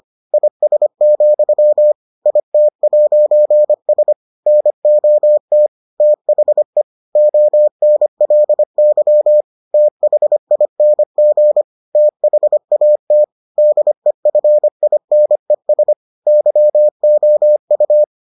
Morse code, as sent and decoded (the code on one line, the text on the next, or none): .. ... --..-- / .. - .----. ... / -. --- - / - .... / --- -. .-.. -.-- / - .... .. -. --. / - .... .- - / -.. ..-. .. -. ... / -.-- --- ..-
IS, IT'S NOT THE ONLY THING THAT DEFINES YOU